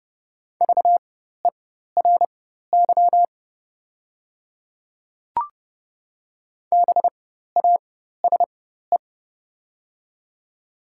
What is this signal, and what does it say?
Morse code, heard as a keyed tone, about 30 wpm, spaced as for 14 wpm, 700 Hz. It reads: VERY BASE
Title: 4x Character Spacing - Top 300 Words - 30wpm